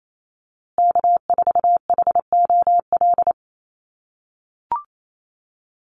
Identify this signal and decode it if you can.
Morse code, heard as a keyed tone, about 28 wpm, 700 Hz. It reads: K4HOL
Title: USA Call Signs Repeated 3x 28wpm